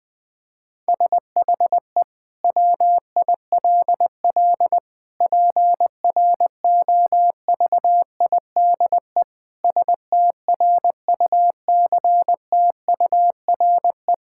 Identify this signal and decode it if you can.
Morse code, heard as a keyed tone, about 20 wpm, 700 Hz. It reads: SHE WILL PROVIDE STRUCTURE
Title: Sentences from Top 1000 Words - Review Part 6 - 20wpm